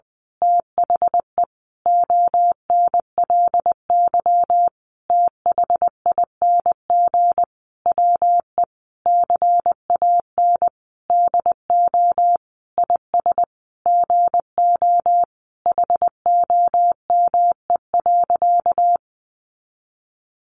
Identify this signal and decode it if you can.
Morse code, heard as a keyed tone, about 20 wpm, 700 Hz. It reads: THE ONLY THING WE CAN DO IS GO HOME.
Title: Sentences from Top 200 Words 20wpm